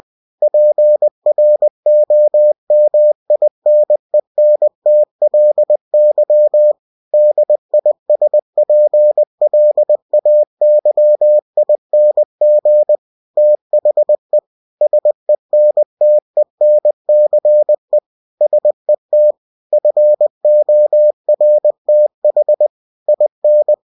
Morse code, heard as a keyed tone, about 20 wpm, 600 Hz. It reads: PROMINENTLY DISPLAYING THE SENTENCE SET FORTH IN